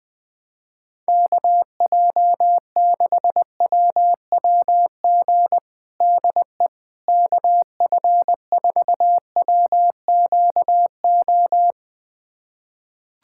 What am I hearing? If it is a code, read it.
Morse code, heard as a keyed tone, about 21 wpm, 700 Hz. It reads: KJ6WWG DE KF4WQO
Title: USA Callsign DE Callsign - 6 Characters - 20wpm